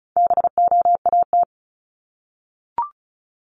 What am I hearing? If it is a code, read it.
Morse code, heard as a keyed tone, about 36 wpm, 700 Hz. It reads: BOAT
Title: Multi-letter Phonemes Repeated 3x 35wpm